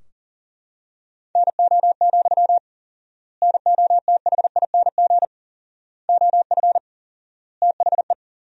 Morse code, heard as a keyed tone, about 40 wpm, 700 Hz. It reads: NO, NOTHING OF THE
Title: 4x Word-Spacing - Sentences from Top 500 Words - 40wpm